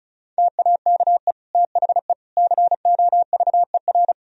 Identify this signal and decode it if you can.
Morse code, heard as a keyed tone, about 36 wpm, 700 Hz. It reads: TAKE THE COVER
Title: Sentences from Top 1000 Words - Review Part 1 - 35wpm